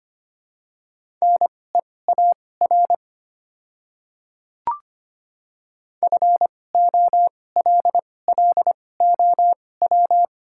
Morse code, heard as a keyed tone, about 25 wpm, spaced as for 18 wpm, 700 Hz. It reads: NEAR FOLLOW
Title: 2x Character Spacing - Top 300 Words - 25wpm